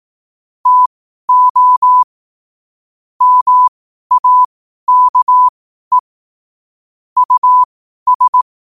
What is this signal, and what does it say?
Morse code, heard as a keyed tone, about 18 wpm, spaced as for 12 wpm, 1000 Hz. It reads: TO MAKE US
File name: qotdfiles / 12wpm